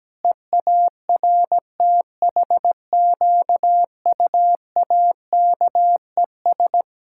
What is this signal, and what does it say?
Morse code, heard as a keyed tone, about 17 wpm, 700 Hz. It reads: EARTHQUAKES
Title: Pangram Challenge v2 - 17wpm